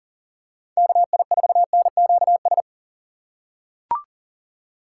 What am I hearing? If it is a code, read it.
Morse code, heard as a keyed tone, about 40 wpm, 700 Hz. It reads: KI4NQS
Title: USA Call Signs - 6 Characters - 2x Character Spacing - 40wpm